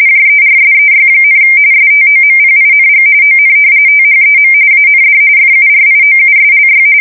RTTY, about 45.45 baud. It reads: JERSEY, STOMPING ON THEM AND FURIOUSLY
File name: fsk170 / battleshipnj